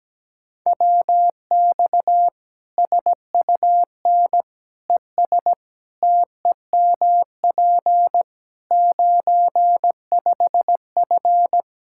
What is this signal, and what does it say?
Morse code, heard as a keyed tone, about 17 wpm, 700 Hz. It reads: WX SUN ES TEMP 95F
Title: Practice QSO - Exchange 2 - Info - 17wpm